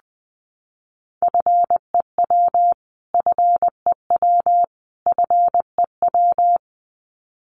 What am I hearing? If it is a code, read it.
Morse code, heard as a keyed tone, about 20 wpm, 700 Hz. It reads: FEW FEW FEW
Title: Top 500 Words Repeated 20wpm